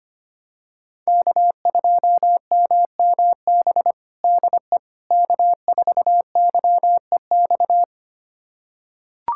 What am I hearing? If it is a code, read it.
Morse code, heard as a keyed tone, about 25 wpm, 700 Hz. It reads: K2MMB DE K4YEX
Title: USA Callsign DE Callsign - 5 Characters - 25wpm